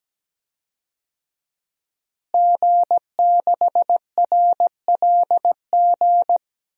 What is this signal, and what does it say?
Morse code, heard as a keyed tone, about 17 wpm, 700 Hz. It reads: G6RLG
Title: DXCC-100 Call Signs - 5 Characters - Repeated 3x - 17wpm